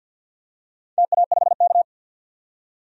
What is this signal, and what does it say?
Morse code, heard as a keyed tone, about 52 wpm, 700 Hz. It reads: TA5K
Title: DXCC-100 Call Signs - 4 Characters - 50wpm